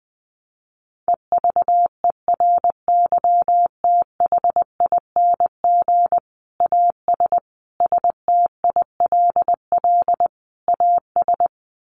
Morse code, heard as a keyed tone, about 20 wpm, 700 Hz. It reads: EVERYTHING AS STILL AS
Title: Sentences from Top 700 Words 20wpm